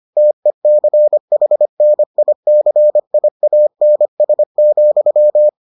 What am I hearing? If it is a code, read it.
Morse code, heard as a keyed tone, about 25 wpm, 600 Hz. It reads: TECHNICIANS,